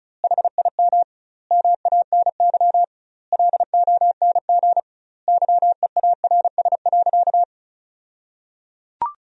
Morse code, heard as a keyed tone, about 34 wpm, 700 Hz. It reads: HIM MANY LONG YEARS.
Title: 2x Word-Spacing - Sentences from Top 200 Words - 35wpm